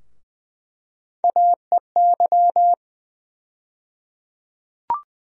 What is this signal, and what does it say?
Morse code, heard as a keyed tone, about 20 wpm, 700 Hz. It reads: AEY
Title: USA Call Sign Postfixes - 3 Letters 20wpm